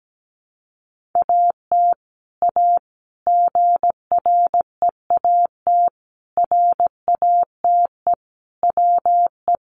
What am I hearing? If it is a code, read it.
Morse code, heard as a keyed tone, about 17 wpm, 700 Hz. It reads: AT A GREAT RATE WE
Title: Sentences from Top 1200 Words 17wpm